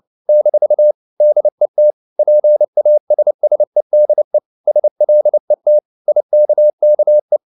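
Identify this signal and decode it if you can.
Morse code, heard as a keyed tone, about 29 wpm, 600 Hz. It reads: - DET PASSEDE SLET IKKE
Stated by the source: Fast (QRQ) Morse at 29 wpm, from CW med Gnister.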